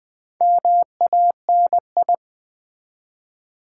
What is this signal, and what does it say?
Morse code, heard as a keyed tone, about 20 wpm, 700 Hz. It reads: MANI
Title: Build A Word - 4 Letters At A Time - 10-Letter Words - 20wpm